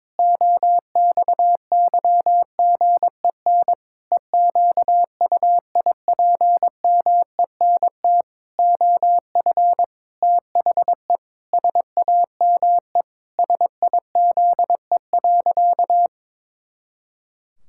Morse code, heard as a keyed tone, about 22 wpm, 700 Hz. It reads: OXYGEN EQUIPMENT OF THE SAME SIZE.
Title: Pangram Challenge v2 - 22wpm